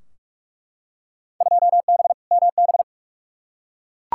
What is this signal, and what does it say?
Morse code, heard as a keyed tone, about 41 wpm, 700 Hz. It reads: 2B MB
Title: Field Day Exchanges Encore - 45wpm